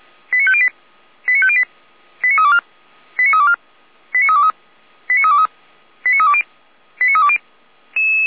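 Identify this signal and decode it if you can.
Five-tone selective call (ZVEI), heard as quick runs of five tones, tones of about 70 ms. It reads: 89508 89508 89324 89324 89323 89323 89320 89320
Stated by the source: ZVEI five-tone selective call, code 89249, received via SDR, fire brigade Orlamünde (feuerwehr-orlamuende).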